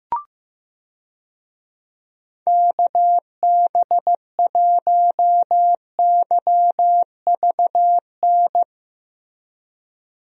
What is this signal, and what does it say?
Morse code, heard as a keyed tone, about 16 wpm, 700 Hz. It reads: KB1YVN
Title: USA Call Signs - 6 Characters - Repeated 2x - 15wpm